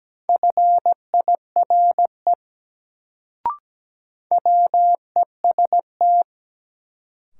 Morse code, heard as a keyed tone, about 17 wpm, 700 Hz. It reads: FIRE WEST
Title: Sentences from Top 400 Words - New Word Drill - 17wpm